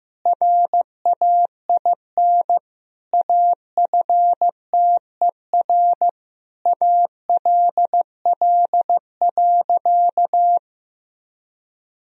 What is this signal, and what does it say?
Morse code, heard as a keyed tone, about 16 wpm, 700 Hz. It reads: RAIN AFTER ALL.
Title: Sentences from Top 1000 Words - Review Part 1 - 15wpm